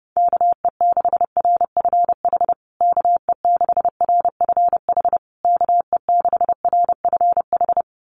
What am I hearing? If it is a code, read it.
Morse code, heard as a keyed tone, about 30 wpm, 700 Hz. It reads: KE6RFH KE6RFH KE6RFH
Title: USA Call Signs Repeated 3x 30wpm